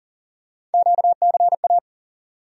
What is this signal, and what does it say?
Morse code, heard as a keyed tone, about 40 wpm, 700 Hz. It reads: QCA
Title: USA Call Sign Postfixes - 3 Letters 40wpm